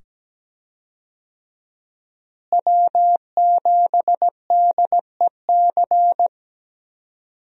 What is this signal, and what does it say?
Morse code, heard as a keyed tone, about 17 wpm, 700 Hz. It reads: W7DEC
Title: USA Call Signs - 5 Characters - Repeated 2x - 17wpm